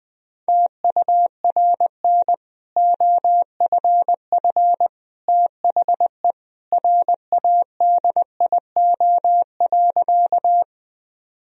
Morse code, heard as a keyed tone, about 20 wpm, 700 Hz. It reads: TURN OFF THE RADIO.
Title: Sentences from Top 1000 Words - Review Part 3 - 20wpm